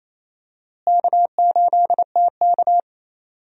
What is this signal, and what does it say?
Morse code, heard as a keyed tone, about 28 wpm, 700 Hz. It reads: K8TK
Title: USA Call Signs - 4 Characters - Repeated 3x - 28wpm